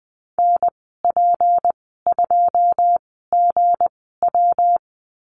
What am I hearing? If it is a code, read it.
Morse code, heard as a keyed tone, about 20 wpm, spaced as for 15 wpm, 700 Hz. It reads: NP2GW
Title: 161 Lesson 26g - Letter g - Call Signs 20wpm